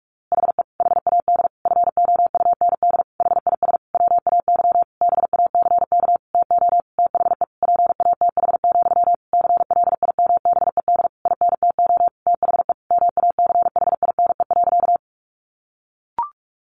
Morse code, heard as a keyed tone, about 46 wpm, 700 Hz. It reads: HE HAD FOUND HIS WAY BACK TO THE PATH, CLIMBED INTO THE MACHINE.
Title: A Sound Of Thunder 45wpm